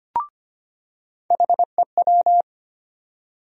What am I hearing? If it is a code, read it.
Morse code, heard as a keyed tone, about 25 wpm, 700 Hz. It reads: HEW
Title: USA Call Sign Postfixes - 3 Letters 25wpm